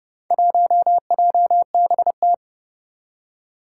Morse code, heard as a keyed tone, about 28 wpm, 700 Hz. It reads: 1JBT
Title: Four Challenging Letter-Number Combinations 30wpm